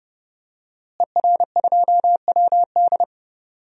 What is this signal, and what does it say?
Morse code, heard as a keyed tone, about 30 wpm, 700 Hz. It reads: ER2WD
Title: DXCC-100 Call Signs - 5 Characters - Repeated 3x - 30wpm